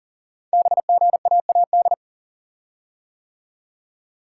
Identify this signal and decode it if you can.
Morse code, heard as a keyed tone, about 42 wpm, 700 Hz. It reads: BGAAD
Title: Code Groups - 5 Letters - 40wpm